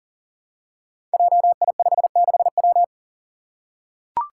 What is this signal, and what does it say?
Morse code, heard as a keyed tone, about 40 wpm, 700 Hz. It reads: JI56W
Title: Five Challenging Letter-Number Combinations 40wpm